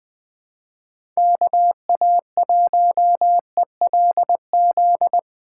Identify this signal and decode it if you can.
Morse code, heard as a keyed tone, about 20 wpm, 700 Hz. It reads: KA1ELZ
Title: USA Call Signs - 6 Characters - 3x Character Spacing - 20wpm